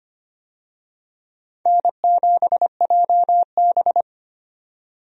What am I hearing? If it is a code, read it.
Morse code, heard as a keyed tone, about 25 wpm, 700 Hz. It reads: N7JB
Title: USA Call Signs - 4 Characters - 25wpm